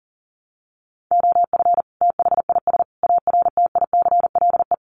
Morse code, heard as a keyed tone, about 40 wpm, 700 Hz. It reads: OF THIS ARTICLE
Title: Sets of 5 Words Encore2 40wpm